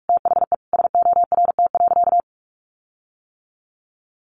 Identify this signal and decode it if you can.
Morse code, heard as a keyed tone, about 44 wpm, 700 Hz. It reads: THE SORT.
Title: Sentences from Top 1500 Words 45wpm